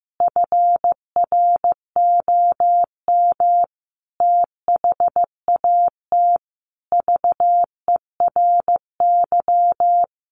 Morse code, heard as a keyed tone, about 15 wpm, 700 Hz. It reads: FROM THAT VERY
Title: Sentences from Top 1500 Words 15wpm